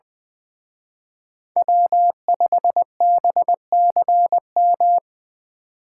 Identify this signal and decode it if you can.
Morse code, heard as a keyed tone, about 23 wpm, 700 Hz. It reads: W5BCM
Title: Speed Racing - USA Call Signs 20wpm